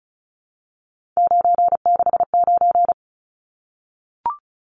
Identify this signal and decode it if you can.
Morse code, heard as a keyed tone, about 34 wpm, 700 Hz. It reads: 969
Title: Numbers - 3-Digit 35wpm